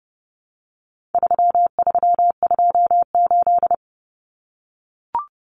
Morse code, heard as a keyed tone, about 30 wpm, 700 Hz. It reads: 3328
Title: Common Cut Numbers - 4-Digit 30wpm